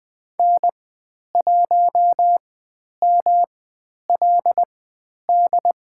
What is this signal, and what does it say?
Morse code, heard as a keyed tone, about 20 wpm, spaced as for 10 wpm, 700 Hz. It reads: N1MLD
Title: 109 Lesson 18g - Letter m - Call Signs 20wpm